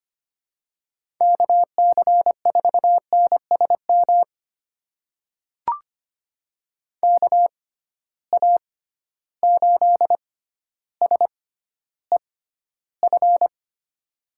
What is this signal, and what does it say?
Morse code, heard as a keyed tone, about 25 wpm, spaced as for 13 wpm, 700 Hz. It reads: KC4NSM KA8SEF
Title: USA Call Signs - 6 Characters - 6x Character Spacing - 25wpm